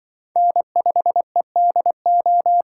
Morse code, heard as a keyed tone, about 23 wpm, 700 Hz. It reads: N5EDO
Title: Speed Racing - USA Call Signs 20wpm